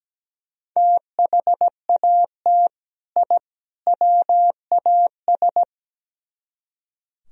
Morse code, heard as a keyed tone, about 17 wpm, 700 Hz. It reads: THAT I WAS
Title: Speed-Racing - Sets of 3 Words 17wpm